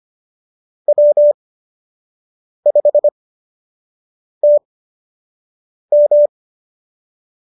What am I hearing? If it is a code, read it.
Morse code, heard as a keyed tone, about 25 wpm, spaced as for 6 wpm, 590 Hz. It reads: W5TM